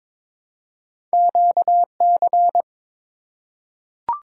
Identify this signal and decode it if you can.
Morse code, heard as a keyed tone, about 22 wpm, 700 Hz. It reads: QC
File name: practice / 22wpm / ARRL Sections v2 - 22wpm